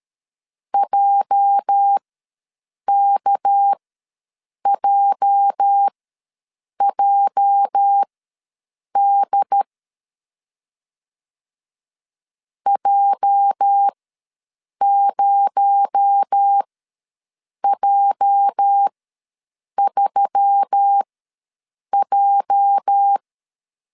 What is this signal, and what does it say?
Morse code, heard as a keyed tone, about 13 wpm, spaced as for 7 wpm, 800 Hz. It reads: JKJJD J0J3J
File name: LEZIONI / 8 YJ45 / 8 YJ ws 12wpm 3342